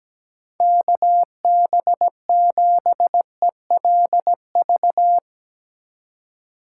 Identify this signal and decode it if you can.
Morse code, heard as a keyed tone, about 17 wpm, 700 Hz. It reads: KB7ELV
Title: USA Call Signs - 6 Characters - 4x Character Spacing - 17wpm